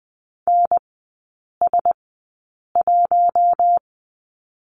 Morse code, heard as a keyed tone, about 20 wpm, spaced as for 8 wpm, 700 Hz. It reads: NS1
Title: 053 Lesson 10e - Letter r - Three Characters 20wpm